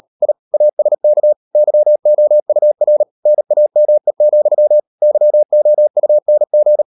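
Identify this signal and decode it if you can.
Morse code, heard as a keyed tone, about 38 wpm, 600 Hz. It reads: I ASK YOUR NAME, YOUNG
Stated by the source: Morse code practice at 38 WPM, sending Ken Follett's novel Never in English, from CW med Gnister.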